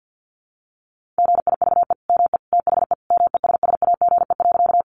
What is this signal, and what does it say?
Morse code, heard as a keyed tone, about 50 wpm, 700 Hz. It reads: GIVE ME THE MESSAGE.
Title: Sentences from Top 1100 Words 50wpm